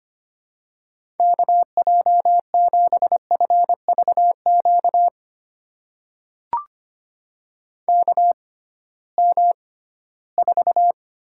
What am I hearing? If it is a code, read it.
Morse code, heard as a keyed tone, about 25 wpm, spaced as for 16 wpm, 700 Hz. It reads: KJ7FVQ KM4
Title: USA Call Signs - 6 Characters - 6x Character Spacing - 25wpm